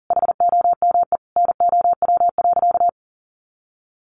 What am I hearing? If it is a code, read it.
Morse code, heard as a keyed tone, about 40 wpm, 700 Hz. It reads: HOME NOW.
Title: Sentences from Top 1300 Words 40wpm